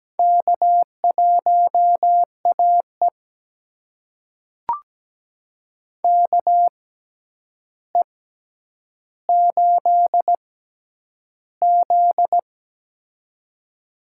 Morse code, heard as a keyed tone, about 17 wpm, spaced as for 9 wpm, 700 Hz. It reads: K1AE KE8Z
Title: USA Call Signs - 4 Characters - 6x Character Spacing - 17wpm